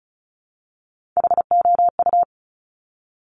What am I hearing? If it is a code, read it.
Morse code, heard as a keyed tone, about 36 wpm, 700 Hz. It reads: HOU
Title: Top Three Letter Combinations in Words 35wpm